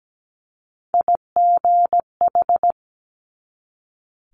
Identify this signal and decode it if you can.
Morse code, heard as a keyed tone, about 17 wpm, 700 Hz. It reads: IGH
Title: Multi-letter Phonemes 17wpm